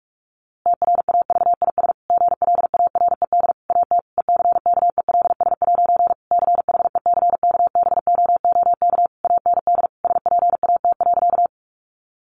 Morse code, heard as a keyed tone, about 46 wpm, 700 Hz. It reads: TRAVIS GLARED AT ECKELS' CHECKBOOK AND SPAT.
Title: A Sound Of Thunder 45wpm